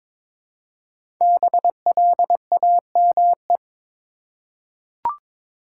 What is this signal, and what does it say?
Morse code, heard as a keyed tone, about 22 wpm, 700 Hz. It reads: BLAME
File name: 5 Letter Words - First 3 Letters Given - 22wpm